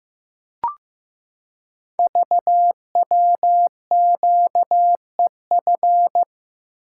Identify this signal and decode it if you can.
Morse code, heard as a keyed tone, about 15 wpm, 700 Hz. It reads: VWQEF
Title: Code Groups - 5 Letters - 15wpm